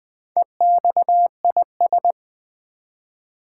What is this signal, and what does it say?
Morse code, heard as a keyed tone, about 20 wpm, 700 Hz. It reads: EXIS
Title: Build A Word - 1 Letter At A Time - 3 to 5-Letter Words - 20wpm